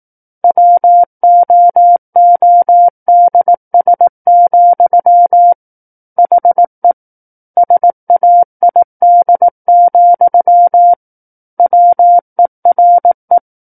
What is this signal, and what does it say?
Morse code, heard as a keyed tone, about 18 wpm, 700 Hz. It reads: WOODS, HE SAID, WERE
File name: 18 WPM